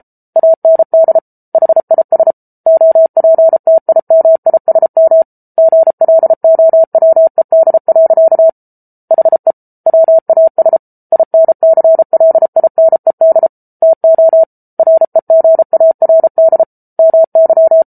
Morse code, heard as a keyed tone, about 34 wpm, 650 Hz. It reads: AND HIS OPTIMISM GLOWED. HE WAS INCLINED TO REGARD MY